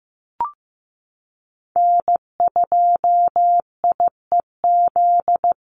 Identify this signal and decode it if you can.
Morse code, heard as a keyed tone, about 14 wpm, 700 Hz. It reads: N2IEZ
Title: USA Call Signs Repeated 3x 15wpm